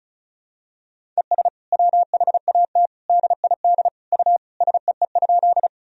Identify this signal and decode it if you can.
Morse code, heard as a keyed tone, about 36 wpm, 700 Hz. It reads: ES WHAT DID U SEE?
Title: Sentences from Top 100 Words and Abbreviations 35wpm